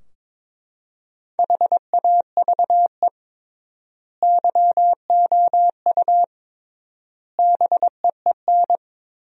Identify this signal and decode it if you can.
Morse code, heard as a keyed tone, about 22 wpm, 700 Hz. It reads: HAVE YOU BEEN
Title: 3x Word-Spacing - Sentences from Top 200 Words - 22wpm